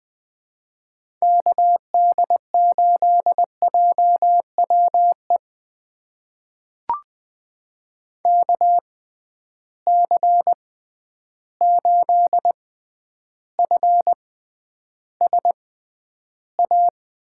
Morse code, heard as a keyed tone, about 20 wpm, spaced as for 10 wpm, 700 Hz. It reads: KD8JWE KC8FSA
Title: USA Call Signs - 6 Characters - 6x Character Spacing - 20wpm